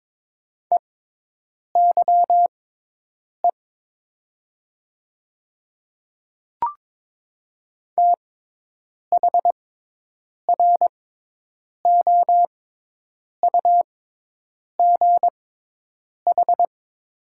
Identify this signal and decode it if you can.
Morse code, heard as a keyed tone, about 22 wpm, spaced as for 8 wpm, 700 Hz. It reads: EYE THROUGH
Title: 6x Character Spacing - Top 300 Words - 22wpm